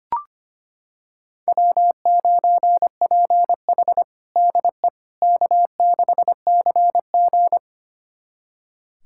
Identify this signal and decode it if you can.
Morse code, heard as a keyed tone, about 25 wpm, 700 Hz. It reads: W9PH DE K6CG
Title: USA Callsign DE Callsign - 4 Characters - 25wpm